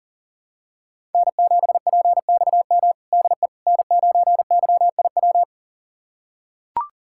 Morse code, heard as a keyed tone, about 40 wpm, 700 Hz. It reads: N7PXM DE N9YIW
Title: USA Callsign DE Callsign - 5 Characters - 40wpm